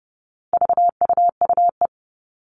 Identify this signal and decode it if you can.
Morse code, heard as a keyed tone, about 30 wpm, 700 Hz. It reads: VUUE
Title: All Cut Numbers - 4-Digit 30wpm